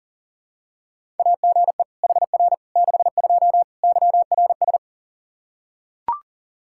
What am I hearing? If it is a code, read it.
Morse code, heard as a keyed tone, about 37 wpm, 700 Hz. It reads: AGE HR 62 YRS
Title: Practice QSO - Exchange 3 - More Info - 40wpm